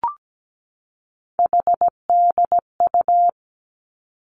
Morse code, heard as a keyed tone, about 17 wpm, 700 Hz. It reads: HDU
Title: Three Challenging Letter-Number Combinations 17wpm